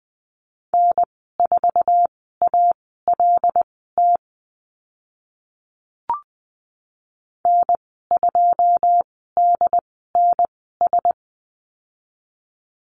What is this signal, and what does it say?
Morse code, heard as a keyed tone, about 20 wpm, spaced as for 15 wpm, 700 Hz. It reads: N4ALT N2DNS
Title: 140 Lesson 23g - Letter f - Call Signs 20wpm